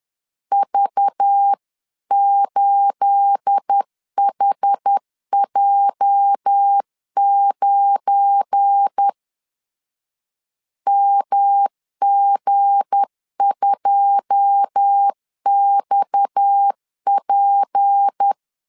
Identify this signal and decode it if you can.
Morse code, heard as a keyed tone, about 10 wpm, 800 Hz. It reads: V8HJ9 MG2XP
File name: LEZIONI / 9 28 10wpm 2842